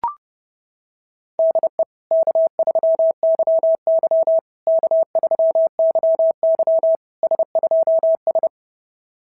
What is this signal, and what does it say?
Morse code, heard as a keyed tone, about 27 wpm, 650 Hz. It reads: DE K3YY K3YY S2S
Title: SOTA QSO Practice 30wpm